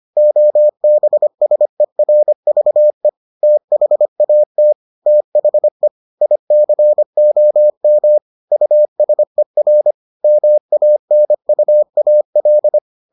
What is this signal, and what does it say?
Morse code, heard as a keyed tone, about 25 wpm, 600 Hz. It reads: OBSERVE THAT THE ICOM USER MANUAL